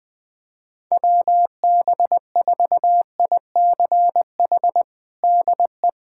Morse code, heard as a keyed tone, about 21 wpm, 700 Hz. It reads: WB4ICH DE
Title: USA Callsign DE Callsign - 6 Characters - 20wpm